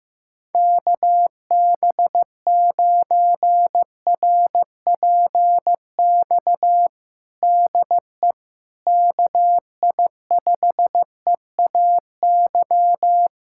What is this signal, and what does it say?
Morse code, heard as a keyed tone, about 15 wpm, 700 Hz. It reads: KB9RPX DE KI5EAY
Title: USA Callsign DE Callsign - 6 Characters - 15wpm